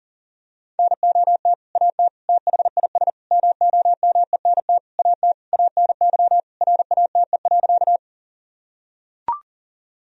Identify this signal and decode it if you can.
Morse code, heard as a keyed tone, about 37 wpm, 700 Hz. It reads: NOT AT THIS MOMENT AT ANY RATE.
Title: Sentences from Top 1000 Words - Review Part 4 - 40wpm